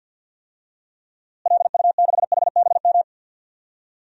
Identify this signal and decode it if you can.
Morse code, heard as a keyed tone, about 53 wpm, 700 Hz. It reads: LU6HBM
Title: DXCC-100 Call Signs - 6 Characters - Repeated 2x - 50wpm